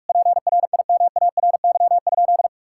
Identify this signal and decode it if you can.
Morse code, heard as a keyed tone, about 46 wpm, 700 Hz. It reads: PRIMARY?
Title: Sentences from Top 1000 Words - Review Part 5 - 45wpm